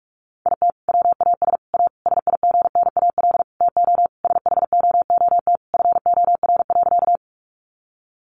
Morse code, heard as a keyed tone, about 44 wpm, 700 Hz. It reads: IT WAS A SIGNAL TO SHOOT FOR.
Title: Sentences from Top 1400 Words 45wpm